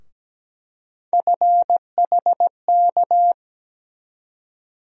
Morse code, heard as a keyed tone, about 17 wpm, 700 Hz. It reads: FHK
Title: USA Call Sign Postfixes - 3 Letters 17wpm